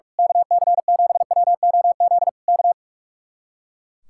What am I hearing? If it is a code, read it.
Morse code, heard as a keyed tone, about 42 wpm, 700 Hz. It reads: KC7WOZ K
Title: Speed-Racing Common QSO Phrases - 45wpm